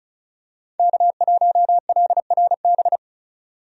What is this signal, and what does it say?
Morse code, heard as a keyed tone, about 36 wpm, 700 Hz. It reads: K1LRB
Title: USA Call Signs - 5 Characters - 35wpm